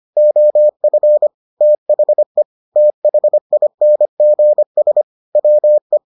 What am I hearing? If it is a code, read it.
Morse code, heard as a keyed tone, about 25 wpm, 600 Hz. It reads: OF THE THINGS WE